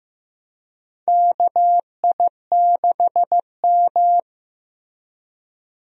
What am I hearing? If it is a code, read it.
Morse code, heard as a keyed tone, about 15 wpm, 700 Hz. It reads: KI6M
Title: USA Call Signs - 4 Characters - Repeated 2x - 15wpm